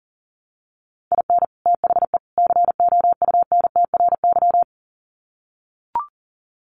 Morse code, heard as a keyed tone, about 38 wpm, 700 Hz. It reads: IN THE COUNTRY
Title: Sets of 3 Words 40wpm